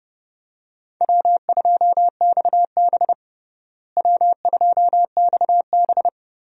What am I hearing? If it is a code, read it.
Morse code, heard as a keyed tone, about 30 wpm, 700 Hz. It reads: W2XB W2XB
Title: USA Call Signs - 4 Characters - Repeated 3x - 30wpm